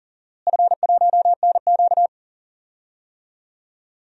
Morse code, heard as a keyed tone, about 40 wpm, 700 Hz. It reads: F1NQ
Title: DXCC-100 Call Signs - 4 Characters - Repeated 2x - 40wpm